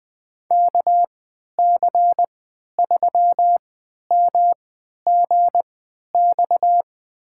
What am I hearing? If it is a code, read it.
Morse code, heard as a keyed tone, about 20 wpm, spaced as for 11 wpm, 700 Hz. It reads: KC3MGX
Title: USA Call Signs - 6 Characters - 3x Character Spacing - 20wpm